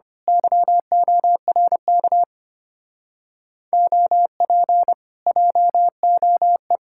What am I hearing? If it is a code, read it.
Morse code, heard as a keyed tone, about 24 wpm, 700 Hz. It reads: YORK OP JOE
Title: Speed-Racing Common QSO Phrases Encore - 25wpm